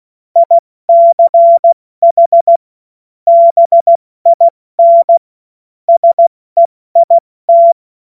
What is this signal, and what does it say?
Morse code, heard as a keyed tone, about 16 wpm, spaced as for 14 wpm, 680 Hz. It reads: ICH BIN SEIT
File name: Witz-16-September-2025-70-F80